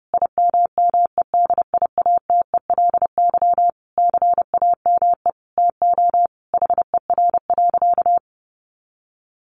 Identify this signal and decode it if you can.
Morse code, heard as a keyed tone, about 30 wpm, 700 Hz. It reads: IMMEDIATELY CAME TO HER.
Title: Sentences from Top 1100 Words 30wpm